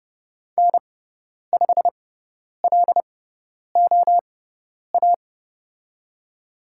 Morse code, heard as a keyed tone, about 29 wpm, spaced as for 10 wpm, 700 Hz. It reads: N5LOA
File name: Lesson 17g - Letter u - Call Signs 30wpm